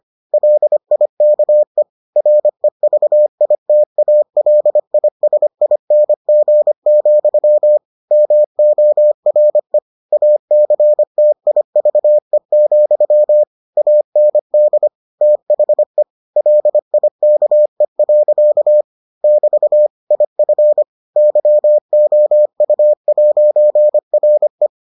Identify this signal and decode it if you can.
Morse code, heard as a keyed tone, about 25 wpm, 600 Hz. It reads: LIKE REVITALISING, MORE ACTIVE, AND THE LIKE. <BT> IF YOU'RE